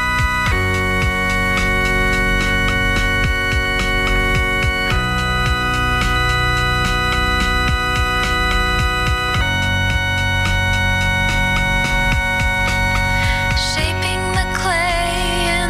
Dial-up modem handshake.